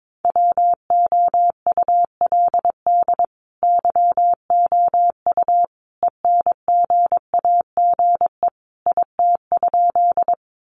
Morse code, heard as a keyed tone, about 22 wpm, 700 Hz. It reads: WOULD YOU ENGAGE IT?
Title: Sentences from Top 1500 Words 22wpm